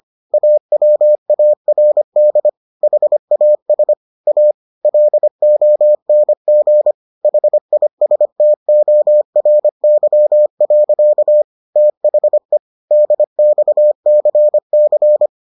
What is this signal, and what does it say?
Morse code, heard as a keyed tone, about 25 wpm, 600 Hz. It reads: AWARD HAS A LONG HISTORY. THE DXCC